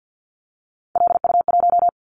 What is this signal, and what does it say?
Morse code, heard as a keyed tone, about 50 wpm, 700 Hz. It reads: LU1
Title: Callsign Prefixes for Top 50 DXCC - 50wpm